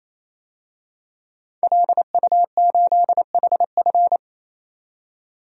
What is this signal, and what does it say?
Morse code, heard as a keyed tone, about 28 wpm, 700 Hz. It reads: LU8HF
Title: DXCC-100 Call Signs - 5 Characters - 28wpm